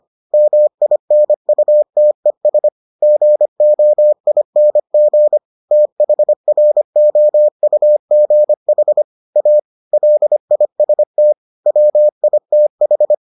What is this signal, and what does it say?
Morse code, heard as a keyed tone, about 25 wpm, 600 Hz. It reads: MINUTES GOING THROUGH A LIST WITH